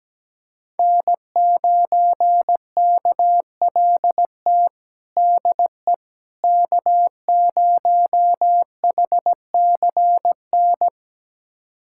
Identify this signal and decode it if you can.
Morse code, heard as a keyed tone, about 17 wpm, 700 Hz. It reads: N9KLT DE K0HCN
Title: USA Callsign DE Callsign - 5 Characters - 17wpm